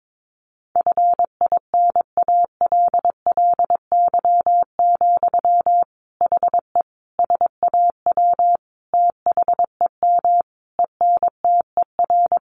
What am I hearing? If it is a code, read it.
Morse code, heard as a keyed tone, about 22 wpm, 700 Hz. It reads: FINALLY, HE SAW THEM ENTER
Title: Sentences from Top 1300 Words 22wpm